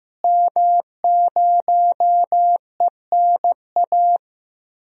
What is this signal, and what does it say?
Morse code, heard as a keyed tone, about 15 wpm, 700 Hz. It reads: M0ENA
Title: DXCC-100 Call Signs - 5 Characters - Repeated 3x - 15wpm